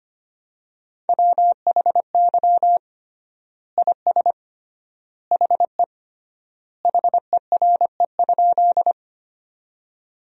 Morse code, heard as a keyed tone, about 25 wpm, 700 Hz. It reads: WHY IS HE HERE?
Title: 3x Word-Spacing - Sentences from Top 200 Words - 25wpm